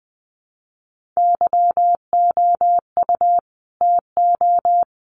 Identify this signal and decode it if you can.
Morse code, heard as a keyed tone, about 20 wpm, 700 Hz. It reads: YOU TO
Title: Sets of 2 Words 20wpm